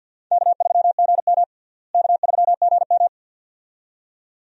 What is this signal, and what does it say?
Morse code, heard as a keyed tone, about 47 wpm, 700 Hz. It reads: K3GM K3GM
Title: USA Call Signs - 4 Characters - Repeated 2x - 50wpm